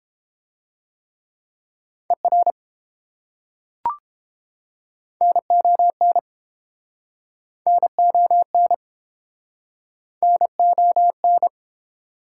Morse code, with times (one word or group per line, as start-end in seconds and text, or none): 2.1-2.5 ER
5.2-6.2 NON
7.7-8.7 NON
10.2-11.5 NON